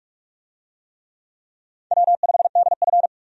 Morse code, heard as a keyed tone, about 46 wpm, 700 Hz. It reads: W5DF
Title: USA Call Signs - 4 Characters - 45wpm